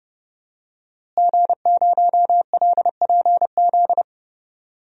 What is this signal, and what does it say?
Morse code, heard as a keyed tone, about 29 wpm, 700 Hz. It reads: G0LPZ
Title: DXCC-100 Call Signs - 5 Characters - Repeated 3x - 30wpm